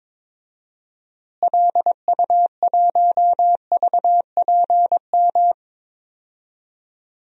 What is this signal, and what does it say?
Morse code, heard as a keyed tone, about 22 wpm, 700 Hz. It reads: LU1VPM
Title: DXCC-100 Call Signs - 6 Characters - Repeated 2x - 22wpm